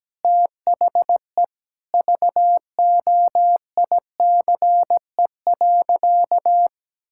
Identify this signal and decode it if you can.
Morse code, heard as a keyed tone, about 17 wpm, 700 Hz. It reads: THE VOICE.
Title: Sentences from Top 1600 Words 17wpm